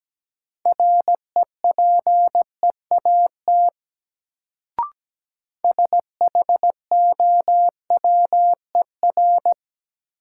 Morse code, heard as a keyed tone, about 17 wpm, 700 Hz. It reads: REPEAT SHOWER
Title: Sentences from Top 1600 Words - New Word Drill - 17wpm